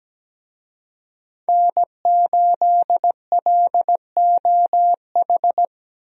Morse code, heard as a keyed tone, about 17 wpm, 700 Hz. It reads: N8LOH
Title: USA Call Signs - 5 Characters - Repeated 2x - 17wpm